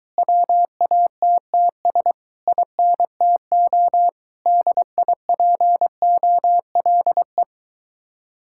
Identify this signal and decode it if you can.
Morse code, heard as a keyed tone, about 23 wpm, 700 Hz. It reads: WATTS INTO DIPOLE